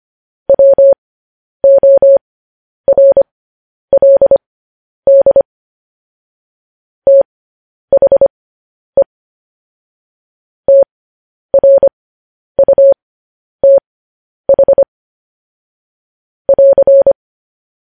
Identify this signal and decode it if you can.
Morse code, heard as a keyed tone, about 25 wpm, spaced as for 10 wpm, 550 Hz. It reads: WORLD THE TRUTH <AR>
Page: news headlines in morse code at 10 wpm